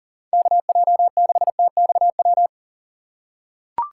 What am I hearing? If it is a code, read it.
Morse code, heard as a keyed tone, about 40 wpm, 700 Hz. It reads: KJ6TXW
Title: USA Call Signs - 6 Characters - 4x Character Spacing - 40wpm